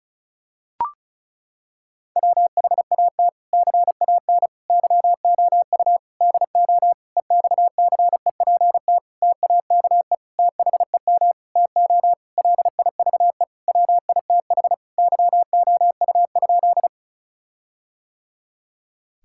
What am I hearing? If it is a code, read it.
Morse code, heard as a keyed tone, about 34 wpm, 700 Hz. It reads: WHAT CAN YOU DO EXCEPT TAKE THEM TO LIVE WITH YOU?
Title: Sentences from Top 1600 Words 35wpm